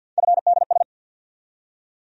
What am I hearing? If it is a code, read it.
Morse code, heard as a keyed tone, about 52 wpm, 700 Hz. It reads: FDS
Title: USA Call Sign Postfixes - 3 Letters 50wpm